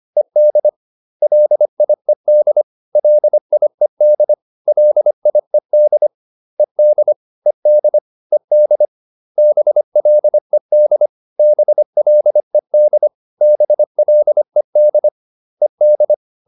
Morse code, heard as a keyed tone, about 25 wpm, 600 Hz. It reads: ED LIED LIED LIED ED ED ED BLED BLED BLED ED